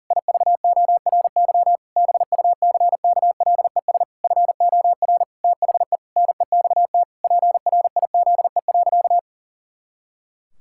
Morse code, heard as a keyed tone, about 40 wpm, 700 Hz. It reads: IVORY BUCKLES FOR THE NEXT PRIZE.
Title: Pangram Challenge v2 - 40wpm